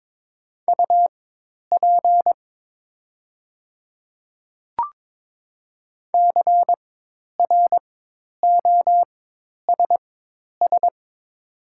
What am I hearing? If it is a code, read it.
Morse code, heard as a keyed tone, about 22 wpm, spaced as for 10 wpm, 700 Hz. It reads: UP CROSS
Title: 4x Character Spacing - Top 500 Words - 22wpm